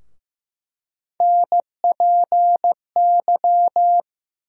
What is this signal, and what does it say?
Morse code, heard as a keyed tone, about 15 wpm, 700 Hz. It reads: NPY
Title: USA Call Sign Postfixes - 3 Letters 15wpm